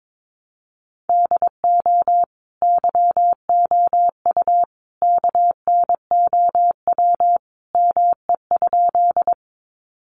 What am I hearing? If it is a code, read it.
Morse code, heard as a keyed tone, about 22 wpm, 700 Hz. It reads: DO YOU KNOW ME?
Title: Sentences from Top 200 Words 22wpm